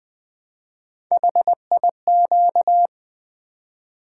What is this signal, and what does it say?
Morse code, heard as a keyed tone, about 20 wpm, 700 Hz. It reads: HIQ
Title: Three Letter-Number Combinations - 20wpm